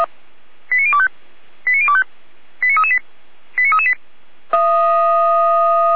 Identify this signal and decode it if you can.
Five-tone selective call (ZVEI), heard as quick runs of five tones, tones of about 70 ms. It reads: 89025 89025 89338 89338